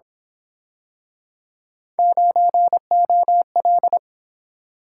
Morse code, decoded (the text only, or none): 9OL